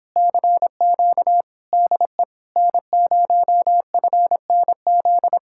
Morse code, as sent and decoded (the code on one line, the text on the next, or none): -.-. --.- / -.. / -. ----- ..-. -. --..
CQ DE N0FNZ